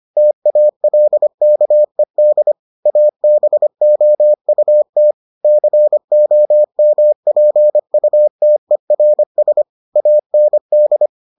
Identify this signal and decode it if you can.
Morse code, heard as a keyed tone, about 25 wpm, 600 Hz. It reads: TALKED ABOUT COMPUTERS AND